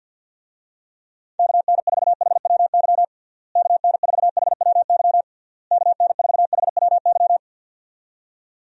Morse code, heard as a keyed tone, about 53 wpm, 700 Hz. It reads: KN4HWY KN4HWY KN4HWY
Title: USA Call Signs - 6 Characters - Repeated 3x - 50wpm